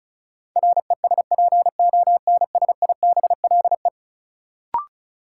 Morse code, heard as a keyed tone, about 36 wpm, 700 Hz. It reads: RESPONSIBLE
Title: Sentences from Top 1000 Words - Review Part 7 - New Word Drill - 35wpm